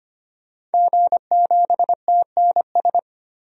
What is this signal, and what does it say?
Morse code, heard as a keyed tone, about 25 wpm, 700 Hz. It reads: G7TNS
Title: DXCC-100 Call Signs - 5 Characters - 25wpm